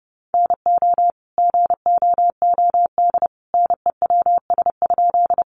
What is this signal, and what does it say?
Morse code, heard as a keyed tone, about 30 wpm, 700 Hz. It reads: NO GOOD NEWS?
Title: Sentences from Top 800 Words 30wpm